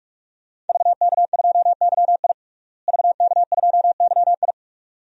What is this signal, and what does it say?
Morse code, heard as a keyed tone, about 45 wpm, 700 Hz. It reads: VK2YI VK2YI
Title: DXCC-100 Call Signs - 5 Characters - Repeated 2x - 45wpm